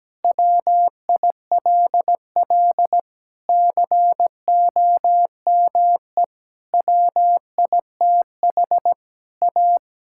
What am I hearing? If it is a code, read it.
Morse code, heard as a keyed tone, about 17 wpm, 700 Hz. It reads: WILL COME WITH A
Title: Sentences from Top 1600 Words 17wpm